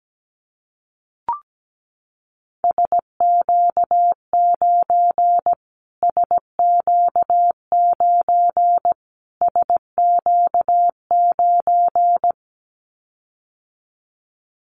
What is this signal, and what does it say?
Morse code, heard as a keyed tone, about 17 wpm, 700 Hz. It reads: SQ9 SQ9 SQ9
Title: Callsign Prefixes for Top 25 DXCC - Repeated 3x - 17wpm